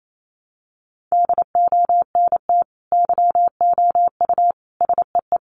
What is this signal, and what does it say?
Morse code, heard as a keyed tone, about 28 wpm, 700 Hz. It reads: DONT YOU SEE
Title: Sentences from Top 300 Words 28wpm